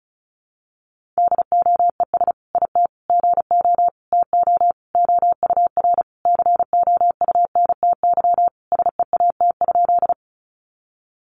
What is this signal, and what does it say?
Morse code, heard as a keyed tone, about 34 wpm, 700 Hz. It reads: DOES IT GO TO OUR COUNTY SEAT?
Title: Sentences from Top 1500 Words 35wpm